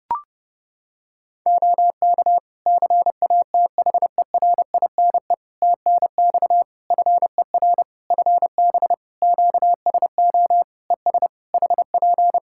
Morse code, heard as a keyed tone, about 30 wpm, 700 Hz. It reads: OK CATHERINE TNX FER FB QSO ES HP
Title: Practice QSO - Exchange 4 - Ending - 30wpm